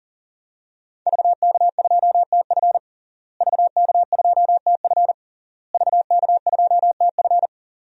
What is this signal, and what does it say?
Morse code, heard as a keyed tone, about 40 wpm, 700 Hz. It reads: VK2TF VK2TF VK2TF
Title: DXCC-100 Call Signs - 5 Characters - Repeated 3x - 40wpm